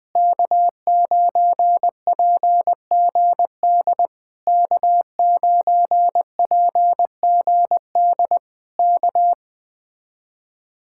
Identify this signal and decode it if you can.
Morse code, heard as a keyed tone, about 20 wpm, 700 Hz. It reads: K9PGD K9PGD K